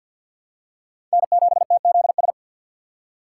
Morse code, heard as a keyed tone, about 51 wpm, 700 Hz. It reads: N7TZS
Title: USA Call Signs - 5 Characters - 3x Character Spacing - 50wpm